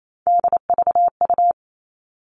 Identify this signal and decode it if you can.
Morse code, heard as a keyed tone, about 28 wpm, 700 Hz. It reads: DVU
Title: All Cut Numbers - 3-Digit 28wpm